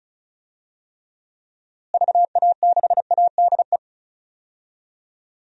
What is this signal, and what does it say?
Morse code, heard as a keyed tone, about 36 wpm, 700 Hz. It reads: VA6ADE
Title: DXCC-100 Call Signs - 6 Characters - Repeated 2x - 35wpm